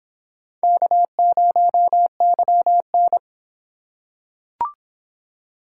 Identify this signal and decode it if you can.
Morse code, heard as a keyed tone, about 26 wpm, 700 Hz. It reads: K0YN